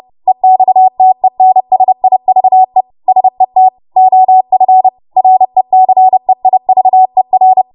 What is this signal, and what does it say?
Morse code, heard as a keyed tone, about 30 wpm, 750 Hz. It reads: EXTENSIVE SET OF RECEIVER
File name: Morse / 30WPM